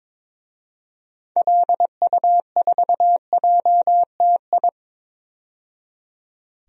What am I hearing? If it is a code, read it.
Morse code, heard as a keyed tone, about 21 wpm, 700 Hz. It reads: LU4JTI
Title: DXCC-100 Call Signs - 6 Characters - 22wpm